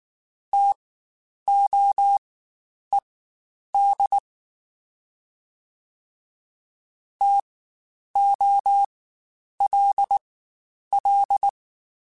Morse code, heard as a keyed tone, about 19 wpm, spaced as for 9 wpm, 800 Hz. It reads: TOED TOLL